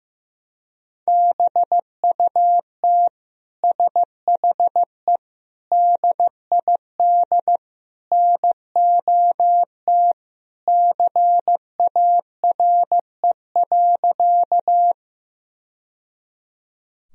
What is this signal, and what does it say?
Morse code, heard as a keyed tone, about 15 wpm, 700 Hz. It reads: BUT SHE DID NOT CARE.
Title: Sentences from Top 1000 Words - Review Part 1 - 15wpm